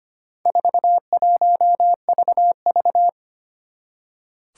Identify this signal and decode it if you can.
Morse code, heard as a keyed tone, about 25 wpm, 700 Hz. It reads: 41VV
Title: Four Challenging Letter-Number Combinations 25wpm